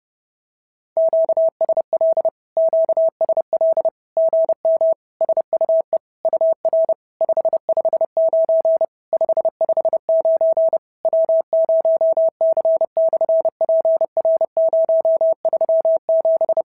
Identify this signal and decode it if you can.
Morse code, heard as a keyed tone, about 30 wpm, 650 Hz. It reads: QSL QSL GM SUE UR 559 559 W0C/PR037
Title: SOTA QSO Practice 30wpm